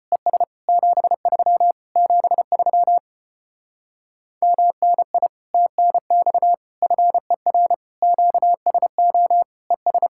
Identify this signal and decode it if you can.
Morse code, heard as a keyed tone, about 33 wpm, 700 Hz. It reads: ES 73 73 MNI TNX FER QSO ES